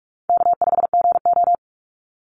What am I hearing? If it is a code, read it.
Morse code, heard as a keyed tone, about 44 wpm, 700 Hz. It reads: K5GO
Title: Call Signs Repeated 3x - 45wpm